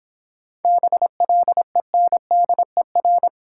Morse code, heard as a keyed tone, about 26 wpm, 700 Hz. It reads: BLENDER